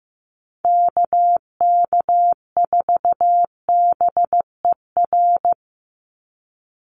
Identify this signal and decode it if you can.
Morse code, heard as a keyed tone, about 15 wpm, 700 Hz. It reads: KK4BER
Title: USA Call Signs 15wpm